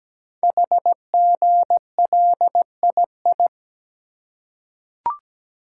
Morse code, heard as a keyed tone, about 17 wpm, 700 Hz. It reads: HGLII
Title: Code Groups - 5 Letters - 17wpm